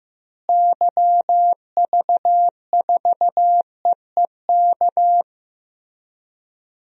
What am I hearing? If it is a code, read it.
Morse code, heard as a keyed tone, about 15 wpm, 700 Hz. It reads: YV4EEK
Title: DXCC-100 Call Signs - 6 Characters - Repeated 2x - 15wpm